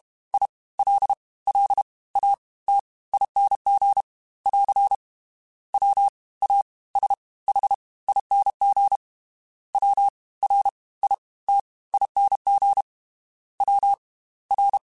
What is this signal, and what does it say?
Morse code, heard as a keyed tone, about 32 wpm, spaced as for 20 wpm, 800 Hz. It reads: ILLATING <AR> WASHING WRITING WR
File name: Special INGs / Special INGs-mp3s / Fast ing @30